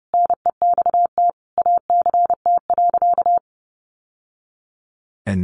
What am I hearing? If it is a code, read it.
Morse code, heard as a keyed tone, about 28 wpm, 700 Hz. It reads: NEXT ACT.
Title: Sentences from Top 900 Words 30wpm